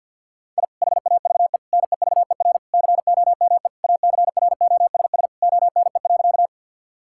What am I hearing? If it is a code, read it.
Morse code, heard as a keyed tone, about 50 wpm, 700 Hz. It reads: I HAVE NEVER COME ACROSS ONE.
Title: Sentences from Top 1000 Words - Review Part 5 - 50wpm